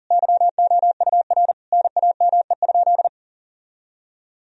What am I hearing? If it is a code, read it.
Morse code, heard as a keyed tone, about 40 wpm, 700 Hz. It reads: YOUR NAME?
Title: Sentences from Top 1000 Words - Review Part 1 - 40wpm